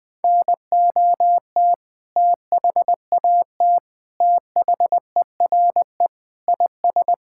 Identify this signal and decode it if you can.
Morse code, heard as a keyed tone, about 20 wpm, 700 Hz. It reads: NOT THAT THERE IS
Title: Sentences with 7-Word Phrases - Spoken Beginning - 20wpm